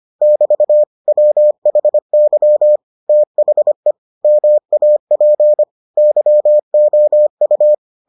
Morse code, heard as a keyed tone, about 25 wpm, 600 Hz. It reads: <BT> WHY THE MAP YOU